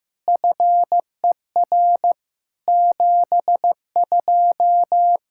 Morse code, heard as a keyed tone, about 15 wpm, 700 Hz. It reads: FER 72
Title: Practice QSO - Exchange 3 - More Info - 15wpm